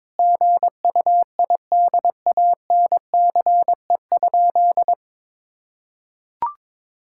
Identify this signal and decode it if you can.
Morse code, heard as a keyed tone, about 22 wpm, 700 Hz. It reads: GUIDANCE?
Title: Sentences from Top 1600 Words 22wpm